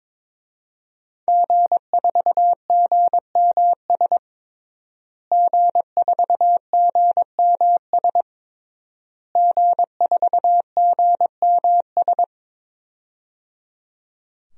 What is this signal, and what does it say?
Morse code, heard as a keyed tone, about 22 wpm, 700 Hz. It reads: G4GMS G4GMS G4GMS
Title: DXCC-100 Call Signs - 5 Characters - Repeated 3x - 22wpm